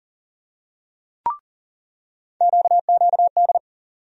Morse code, heard as a keyed tone, about 40 wpm, 700 Hz. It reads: QQD
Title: USA Call Sign Postfixes - 3 Letters 40wpm